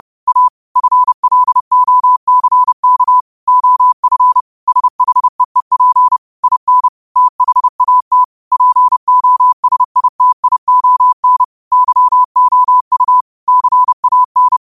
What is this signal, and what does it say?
Morse code, heard as a keyed tone, about 30 wpm, 1000 Hz. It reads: A FLOCK OF SHEEP IN THAT POSITION YOU CAN
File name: qotdfiles / 30wpm